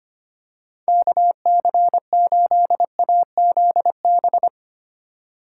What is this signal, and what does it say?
Morse code, heard as a keyed tone, about 25 wpm, 700 Hz. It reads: KC8AZB
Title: USA Call Signs - 6 Characters - 4x Character Spacing - 25wpm